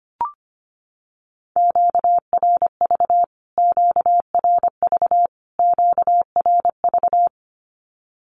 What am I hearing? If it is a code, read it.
Morse code, heard as a keyed tone, about 25 wpm, 700 Hz. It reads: QRV QRV QRV
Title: Q-Codes Repeated 3x 25wpm